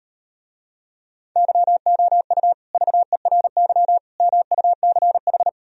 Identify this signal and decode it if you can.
Morse code, heard as a keyed tone, about 38 wpm, 700 Hz. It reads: YOU VERY MUCH